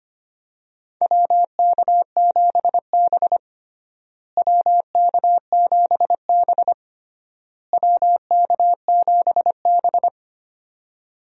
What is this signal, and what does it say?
Morse code, heard as a keyed tone, about 25 wpm, 700 Hz. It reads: WK7B WK7B WK7B
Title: USA Call Signs - 4 Characters - Repeated 3x - 25wpm